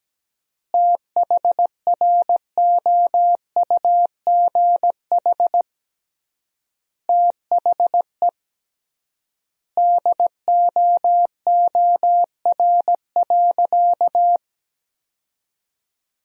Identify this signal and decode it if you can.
Morse code, heard as a keyed tone, about 17 wpm, 700 Hz. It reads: THROUGH THE DOOR.
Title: 3x Word-Spacing - Sentences from Top 400 Words - 17wpm